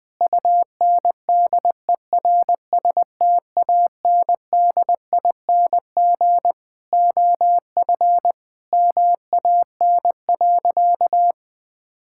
Morse code, heard as a keyed tone, about 21 wpm, 700 Hz. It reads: UNDERSTANDING OF MAN.
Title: Sentences from Top 1000 Words - Review Part 5 - 20wpm